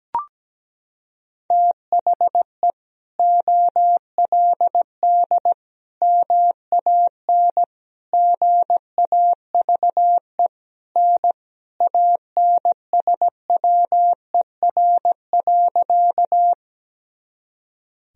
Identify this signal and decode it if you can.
Morse code, heard as a keyed tone, about 17 wpm, 700 Hz. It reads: THE OLD MAN GAVE N ANSWER.
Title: Sentences from Top 500 Words and Abbreviations 17wpm